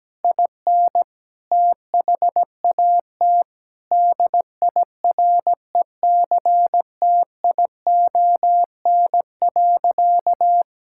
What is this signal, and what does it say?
Morse code, heard as a keyed tone, about 17 wpm, 700 Hz. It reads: IN THAT DIRECTION.
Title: Sentences from Top 1000 Words - Review Part 5 - 17wpm